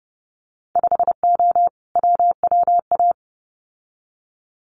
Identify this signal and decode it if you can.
Morse code, heard as a keyed tone, about 30 wpm, 700 Hz. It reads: 5O WWA
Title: Winter Field Day Exchanges Encore 30wpm